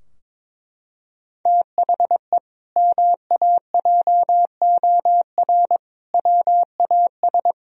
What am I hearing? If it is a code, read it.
Morse code, heard as a keyed tone, about 22 wpm, 700 Hz. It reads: THE MAJOR WAS
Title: Sentences from Top 1000 Words - Review Part 4 - 22wpm